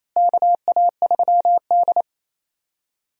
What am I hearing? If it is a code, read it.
Morse code, heard as a keyed tone, about 28 wpm, 700 Hz. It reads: KA3D
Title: USA Call Signs - 4 Characters - Repeated 2x - 28wpm